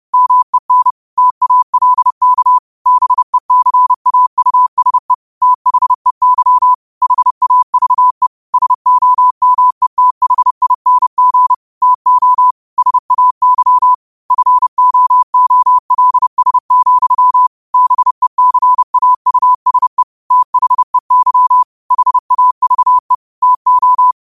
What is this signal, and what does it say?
Morse code, heard as a keyed tone, about 30 wpm, 1000 Hz. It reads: MEN TALK BECAUSE THEY HAVE SOMETHING TO SAY FOOLS, BECAUSE THEY HAVE TO
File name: qotdfiles / 30wpm